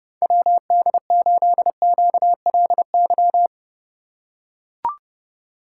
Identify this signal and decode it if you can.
Morse code, heard as a keyed tone, about 30 wpm, 700 Hz. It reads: WD8QLY